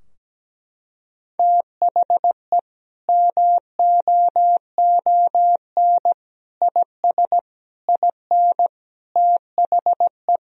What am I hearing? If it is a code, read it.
Morse code, heard as a keyed tone, about 17 wpm, 700 Hz. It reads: THE MOON IS IN THE
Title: Sentences from Top 1000 Words - Review Part 3 - 17wpm